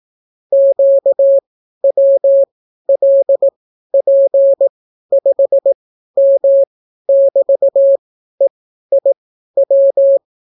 Morse code, heard as a keyed tone, about 18 wpm, spaced as for 12 wpm, 550 Hz. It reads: QWLP5M<BT>EIW